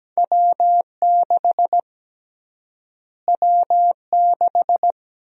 Morse code, heard as a keyed tone, about 17 wpm, 700 Hz. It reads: W6 W6
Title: USA Call Sign Prefixes - Repeated 2x - 17wpm